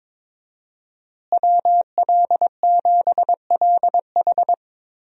Morse code, heard as a keyed tone, about 22 wpm, 700 Hz. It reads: WL7LH
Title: USA Call Signs - 5 Characters - Repeated 2x - 22wpm